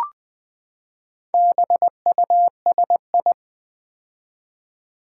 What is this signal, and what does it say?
Morse code, heard as a keyed tone, about 20 wpm, 700 Hz. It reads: BUSI
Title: Build A Word - 4 Letters At A Time - 11-Letter Words - 20wpm